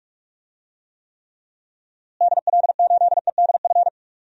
Morse code, heard as a keyed tone, about 46 wpm, 700 Hz. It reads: DL8EDF
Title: DXCC-100 Call Signs - 6 Characters - Repeated 3x - 45wpm